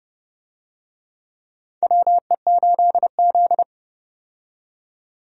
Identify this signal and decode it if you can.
Morse code, heard as a keyed tone, about 30 wpm, 700 Hz. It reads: WE8Z
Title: USA Call Signs - 4 Characters - 30wpm